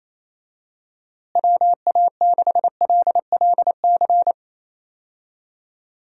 Morse code, heard as a keyed tone, about 27 wpm, 700 Hz. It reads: WA6LLC
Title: USA Call Signs - 6 Characters - Repeated 3x - 28wpm